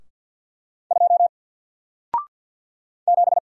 Morse code, heard as a keyed tone, about 48 wpm, 700 Hz. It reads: 2 7
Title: Single Character (Letter:Number:Punctuation:Prosign) 50wpm